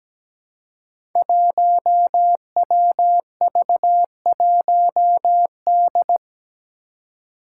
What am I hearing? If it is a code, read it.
Morse code, heard as a keyed tone, about 17 wpm, 700 Hz. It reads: 1WV1D
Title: Five Challenging Letter-Number Combinations 17wpm